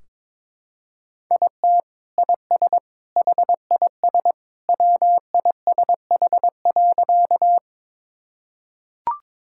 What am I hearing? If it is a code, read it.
Morse code, heard as a keyed tone, about 22 wpm, 700 Hz. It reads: IT IS HIS WISH.
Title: Sentences from Top 1600 Words 22wpm